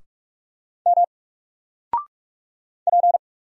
Morse code, heard as a keyed tone, about 44 wpm, 700 Hz. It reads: M P
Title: Single Letter-Number 45wpm